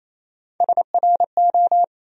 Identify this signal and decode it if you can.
Morse code, heard as a keyed tone, about 27 wpm, 700 Hz. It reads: SRO
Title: USA Call Sign Postfixes - 3 Letters 28wpm